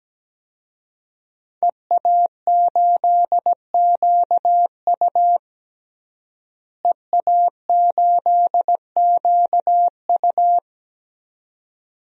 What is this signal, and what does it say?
Morse code, heard as a keyed tone, about 17 wpm, 700 Hz. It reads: EA8QU EA8QU
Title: DXCC-100 Call Signs - 5 Characters - Repeated 2x - 17wpm